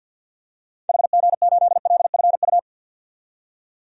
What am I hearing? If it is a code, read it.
Morse code, heard as a keyed tone, about 48 wpm, 700 Hz. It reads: HG8LFU
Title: DXCC-100 Call Signs - 6 Characters - 50wpm